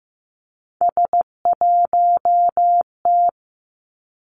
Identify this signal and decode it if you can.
Morse code, heard as a keyed tone, about 15 wpm, 700 Hz. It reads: S1T
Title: Three Challenging Letter-Number Combinations 15wpm